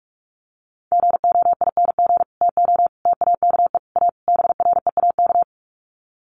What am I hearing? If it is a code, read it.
Morse code, heard as a keyed tone, about 44 wpm, 700 Hz. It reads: GOING TO TAKE A BREAK
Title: Sets of 5 Words 45wpm